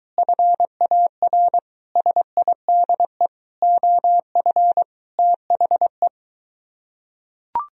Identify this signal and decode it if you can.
Morse code, heard as a keyed tone, about 23 wpm, 700 Hz. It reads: FAR SIDE OF THE